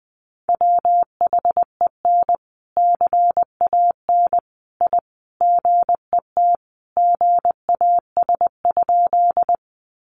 Morse code, heard as a keyed tone, about 20 wpm, 700 Hz. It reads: WHEN CAN I GET GAS?
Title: Sentences from Top 1100 Words 20wpm